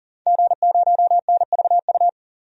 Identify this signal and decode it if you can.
Morse code, heard as a keyed tone, about 40 wpm, 700 Hz. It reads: G0NVU
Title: DXCC-100 Call Signs - 5 Characters - 40wpm